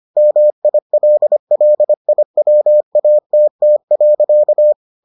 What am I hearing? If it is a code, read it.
Morse code, heard as a keyed tone, about 25 wpm, 600 Hz. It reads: MILLIWATT.